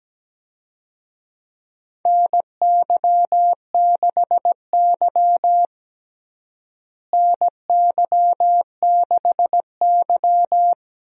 Morse code, heard as a keyed tone, about 17 wpm, 700 Hz. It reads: NY6Y NY6Y
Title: USA Call Signs - 4 Characters - Repeated 3x - 17wpm